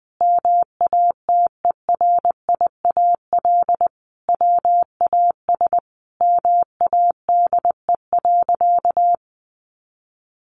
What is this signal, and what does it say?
Morse code, heard as a keyed tone, about 20 wpm, 700 Hz. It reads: MATERIAL WAS MADE.
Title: Sentences from Top 1200 Words 20wpm